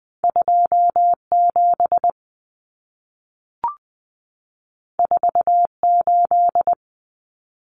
Morse code, heard as a keyed tone, about 20 wpm, 700 Hz. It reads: 27 48
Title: Common Cut Numbers - 2-Digit 20wpm